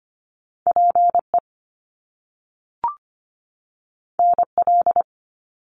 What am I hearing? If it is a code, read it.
Morse code, heard as a keyed tone, about 25 wpm, 700 Hz. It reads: PE NL
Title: Canadian Provinces and Territories 25wpm